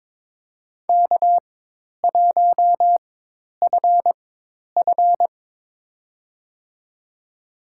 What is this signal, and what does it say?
Morse code, heard as a keyed tone, about 22 wpm, spaced as for 10 wpm, 700 Hz. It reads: K1FF
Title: USA Call Signs - 4 Characters - 4x Character Spacing - 22wpm